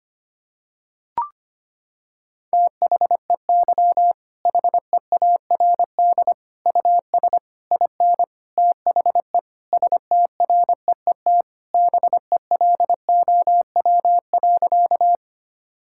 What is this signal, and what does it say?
Morse code, heard as a keyed tone, about 25 wpm, 700 Hz. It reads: THEY HEARD US IN THE STREET BELOW.
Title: Sentences from Top 1000 Words - Review Part 5 - 25wpm